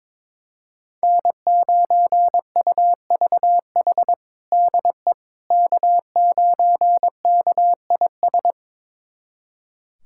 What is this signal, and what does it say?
Morse code, heard as a keyed tone, about 22 wpm, 700 Hz. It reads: N9UVH DE K9KIS
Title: USA Callsign DE Callsign - 5 Characters - 22wpm